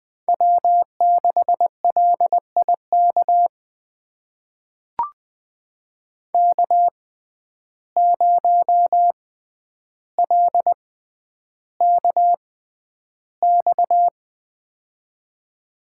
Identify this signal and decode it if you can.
Morse code, heard as a keyed tone, about 20 wpm, spaced as for 10 wpm, 700 Hz. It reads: W6LIK K0LKX
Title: USA Call Signs - 5 Characters - 6x Character Spacing - 20wpm